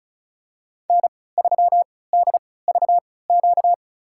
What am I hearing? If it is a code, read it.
Morse code, heard as a keyed tone, about 36 wpm, spaced as for 20 wpm, 700 Hz. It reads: N3DVQ
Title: USA Call Signs - 3x Character Spacing - 35wpm